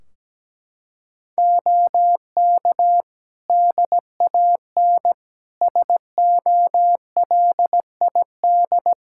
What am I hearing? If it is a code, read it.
Morse code, heard as a keyed tone, about 17 wpm, 700 Hz. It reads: OK DAN SOLID
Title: Practice QSO - Exchange 3 - More Info - 17wpm